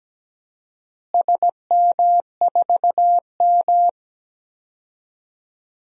Morse code, heard as a keyed tone, about 17 wpm, 700 Hz. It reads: SM4M
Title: DXCC-100 Call Signs - 4 Characters - Repeated 2x - 17wpm